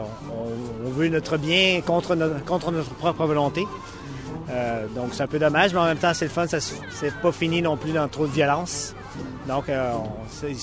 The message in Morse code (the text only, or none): none